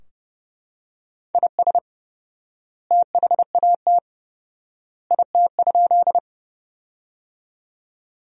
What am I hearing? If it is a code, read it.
Morse code, heard as a keyed tone, about 30 wpm, 700 Hz. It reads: IS THAT IT?
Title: 4x Word-Spacing - Sentences from Top 100 Words - 30wpm